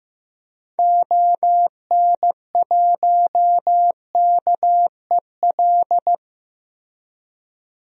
Morse code, heard as a keyed tone, about 15 wpm, 700 Hz. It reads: ON1KEL
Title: DXCC-100 Call Signs - 6 Characters - Repeated 3x - 15wpm